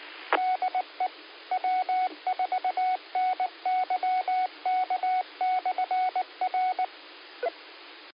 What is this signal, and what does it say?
Morse code, heard as a keyed tone, about 19 wpm, 740 Hz. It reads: DE W4NYK/R
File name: qsorec SimplexLogic 16-31-10